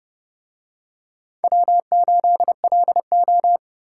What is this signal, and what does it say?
Morse code, heard as a keyed tone, about 29 wpm, 700 Hz. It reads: W8LO